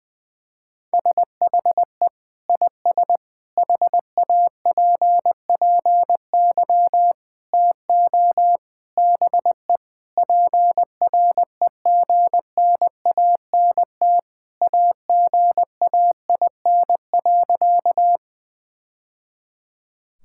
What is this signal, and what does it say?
Morse code, heard as a keyed tone, about 20 wpm, 700 Hz. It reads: SHE IS HAPPY TO BE PREGNANT AGAIN.
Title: Sentences from Top 1000 Words - Review Part 6 - 20wpm